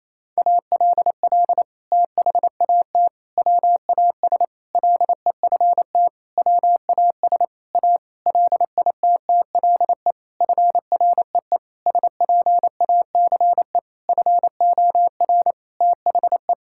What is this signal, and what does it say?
Morse code, heard as a keyed tone, about 27 wpm, 700 Hz. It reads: ALL THAT WAS LEFT WAS A LITTLE FREE SPACE FOR THE
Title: Sentences from Top 1000 Words - Review Part 5 - 28wpm